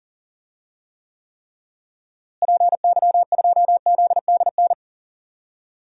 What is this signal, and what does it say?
Morse code, heard as a keyed tone, about 40 wpm, 700 Hz. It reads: PY2ZDN
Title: DXCC-100 Call Signs - 6 Characters - Repeated 3x - 40wpm